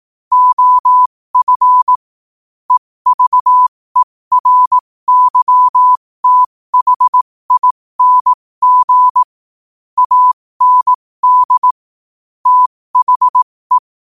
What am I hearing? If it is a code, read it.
Morse code, heard as a keyed tone, about 18 wpm, spaced as for 15 wpm, 1000 Hz. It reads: OF EVERYTHING AND THE
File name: qotdfiles / 15wpm